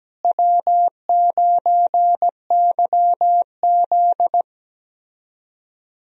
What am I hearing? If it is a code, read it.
Morse code, heard as a keyed tone, about 17 wpm, 700 Hz. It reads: W9YZ
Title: USA Call Signs - 4 Characters - Repeated 3x - 17wpm